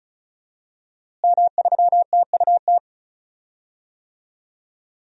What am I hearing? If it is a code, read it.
Morse code, heard as a keyed tone, about 36 wpm, 700 Hz. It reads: M3TUT
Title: DXCC-100 Call Signs - 5 Characters - Repeated 2x - 35wpm